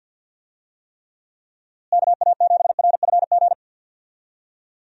Morse code, heard as a keyed tone, about 52 wpm, 700 Hz. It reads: KA7RFG
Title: USA Call Signs - 6 Characters - Repeated 2x - 50wpm